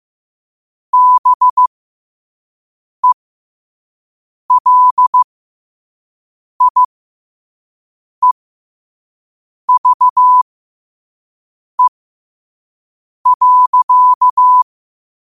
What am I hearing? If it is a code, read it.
Morse code, heard as a keyed tone, about 15 wpm, spaced as for 5 wpm, 1000 Hz. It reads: BELIEVE.
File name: qotdfiles / 5wpm